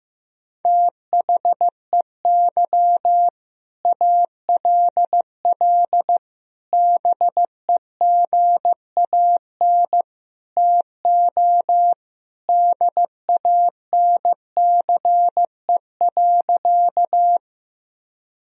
Morse code, heard as a keyed tone, about 16 wpm, 700 Hz. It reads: THEY ALL BEGAN TO DANCE.
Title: Sentences from Top 1000 Words - Review Part 4 - 15wpm